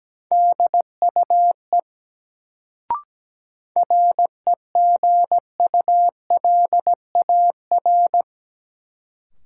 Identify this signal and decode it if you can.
Morse code, heard as a keyed tone, about 17 wpm, 700 Hz. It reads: DUE REGULAR
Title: Sentences from Top 900 Words - New Word Drill - 17wpm